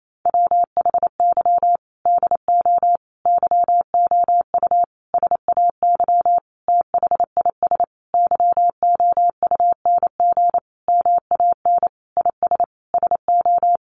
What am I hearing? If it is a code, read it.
Morse code, heard as a keyed tone, about 28 wpm, 700 Hz. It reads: WHY DO YOU SAY THIS YOUNG MAN IS SO
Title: Sentences from Top 1100 Words 28wpm